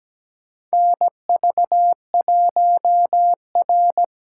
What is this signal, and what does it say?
Morse code, heard as a keyed tone, about 17 wpm, 700 Hz. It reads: NV1R
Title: USA Call Signs - 4 Characters - 4x Character Spacing - 17wpm